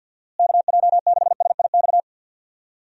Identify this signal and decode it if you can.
Morse code, heard as a keyed tone, about 46 wpm, 700 Hz. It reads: KJ6SIX
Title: USA Call Signs - 6 Characters - Repeated 2x - 50wpm